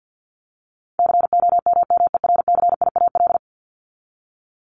Morse code, heard as a keyed tone, about 50 wpm, 700 Hz. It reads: COMMERCIAL
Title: Top 501-750 Words 50wpm